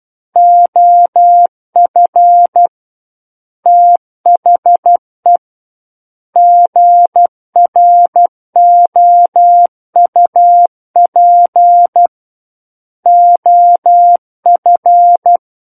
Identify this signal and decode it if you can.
Morse code, heard as a keyed tone, about 12 wpm, 700 Hz. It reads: OF THE GROUP OF
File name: Book One / 03-Chapter 3 - 12 WPM